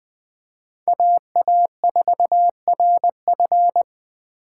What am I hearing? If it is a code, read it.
Morse code, heard as a keyed tone, about 21 wpm, 700 Hz. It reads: AA4RF
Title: 2023 POTA Activator Call Signs - 20wpm